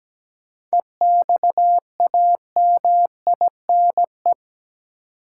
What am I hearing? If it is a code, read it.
Morse code, heard as a keyed tone, about 17 wpm, 700 Hz. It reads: EXAMINE
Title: Seven Letter Words - 17wpm